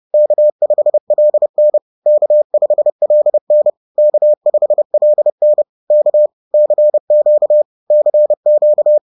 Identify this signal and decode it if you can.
Morse code, heard as a keyed tone, about 30 wpm, 600 Hz. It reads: K5LN K5LN K5LN K CQ CQ
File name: 30wpm-4